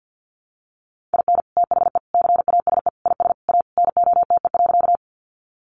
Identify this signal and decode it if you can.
Morse code, heard as a keyed tone, about 50 wpm, 700 Hz. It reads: IN THE CASE IS A NOTE.
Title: Sentences from Top 700 Words 50wpm